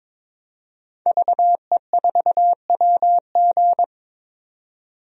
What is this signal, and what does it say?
Morse code, heard as a keyed tone, about 22 wpm, 700 Hz. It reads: VE4WG
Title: DXCC-100 Call Signs - 5 Characters - 22wpm